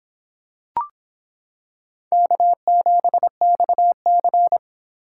Morse code, heard as a keyed tone, about 25 wpm, 700 Hz. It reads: K7XC